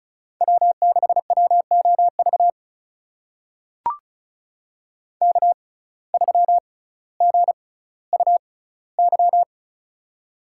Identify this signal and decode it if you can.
Morse code, heard as a keyed tone, about 36 wpm, spaced as for 18 wpm, 700 Hz. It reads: W6WOV K3GUY
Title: USA Call Signs - 5 Characters - 6x Character Spacing - 35wpm